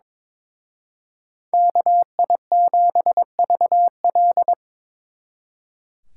Morse code, heard as a keyed tone, about 25 wpm, 700 Hz. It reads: KI7VL
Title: Speed Racing - USA 2x2 Call Signs 22wpm